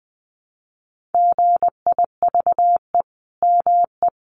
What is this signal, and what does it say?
Morse code, heard as a keyed tone, about 19 wpm, 700 Hz. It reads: GIVE ME
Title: Sentences from Top 1500 Words 20wpm